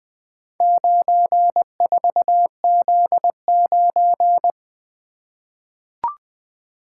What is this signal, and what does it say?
Morse code, heard as a keyed tone, about 20 wpm, 700 Hz. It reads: 94Z9
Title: Four Letter-Number Combinations - 20wpm